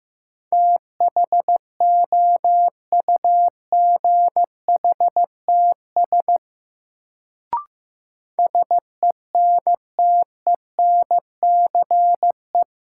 Morse code, heard as a keyed tone, about 15 wpm, 700 Hz. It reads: THOUGHTS SENTENCE
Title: Sentences from Top 1000 Words - Review Part 1 - New Word Drill - 15wpm